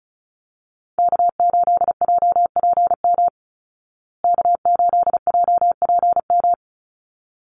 Morse code, heard as a keyed tone, about 34 wpm, 700 Hz. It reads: K8JPM K8JPM
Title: Call Signs Repeated 3x - 35wpm